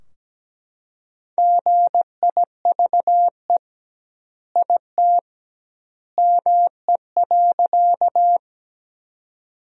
Morse code, heard as a keyed tone, about 17 wpm, 700 Hz. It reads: GIVE IT ME.
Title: 2x Word-Spacing - Sentences from Top 200 Words - 17wpm